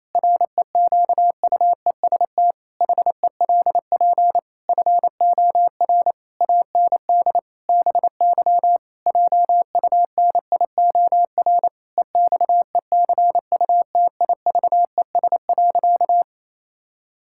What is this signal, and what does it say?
Morse code, heard as a keyed tone, about 27 wpm, 700 Hz. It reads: REQUEST HELP FOR AND BY JUNIOR EXECUTIVES.
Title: Pangram Challenge v2 - 28wpm